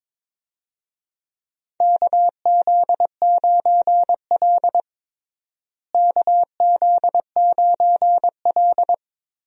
Morse code, heard as a keyed tone, about 23 wpm, 700 Hz. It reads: KZ9L KZ9L
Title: USA Call Signs - 4 Characters - Repeated 2x - 22wpm